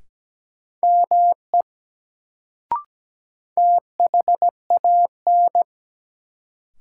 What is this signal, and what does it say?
Morse code, heard as a keyed tone, about 17 wpm, 700 Hz. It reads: ME THAN
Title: Sets of 2 Words - New Word Drill - 17wpm